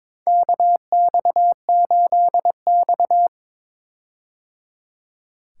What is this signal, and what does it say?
Morse code, heard as a keyed tone, about 22 wpm, 700 Hz. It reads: KX8X
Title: K1USN SST QSO Practice - 22wpm